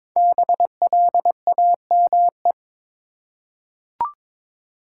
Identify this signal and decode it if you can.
Morse code, heard as a keyed tone, about 22 wpm, 700 Hz. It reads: BLAME
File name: 5 Letter Words - First 3 Letters Given - 22wpm